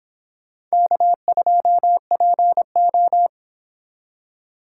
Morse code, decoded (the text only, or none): K2PO